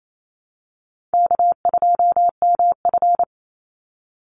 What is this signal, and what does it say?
Morse code, heard as a keyed tone, about 28 wpm, 700 Hz. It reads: K2MF
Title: Easy Call Signs Repeated 2x - 28wpm